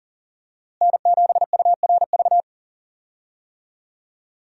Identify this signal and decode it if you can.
Morse code, heard as a keyed tone, about 40 wpm, 700 Hz. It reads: N7URV
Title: USA Call Signs - 5 Characters - Repeated 3x - 40wpm